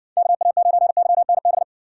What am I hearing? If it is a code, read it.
Morse code, heard as a keyed tone, about 54 wpm, 700 Hz. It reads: KA9YNB